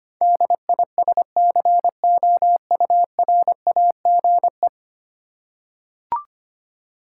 Morse code, heard as a keyed tone, about 25 wpm, 700 Hz. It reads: DISCOURAGE
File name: 10 Letter Words - First 3 Letters Given - 25wpm